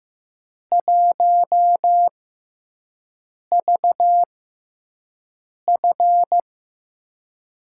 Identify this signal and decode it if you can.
Morse code, heard as a keyed tone, about 15 wpm, spaced as for 5 wpm, 700 Hz. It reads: 1VF
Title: USA Call Signs - 6x Character Spacing - 15wpm